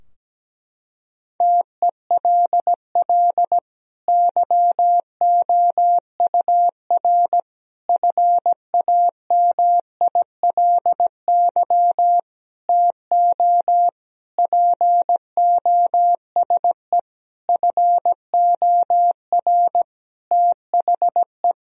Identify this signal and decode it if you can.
Morse code, heard as a keyed tone, about 17 wpm, 700 Hz. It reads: TELL YOUR FAMILY TO POSE FOR THE
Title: Sentences from Top 1000 Words - Review Part 2 - 17wpm